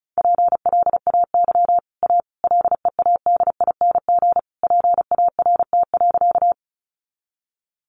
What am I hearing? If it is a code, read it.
Morse code, heard as a keyed tone, about 34 wpm, 700 Hz. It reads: PLAY A LEADING PART.
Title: Sentences from Top 1100 Words 35wpm